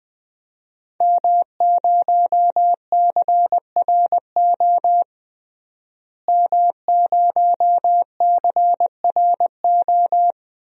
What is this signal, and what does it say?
Morse code, heard as a keyed tone, about 21 wpm, 700 Hz. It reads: M0CRO M0CRO
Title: DXCC-100 Call Signs - 5 Characters - Repeated 2x - 20wpm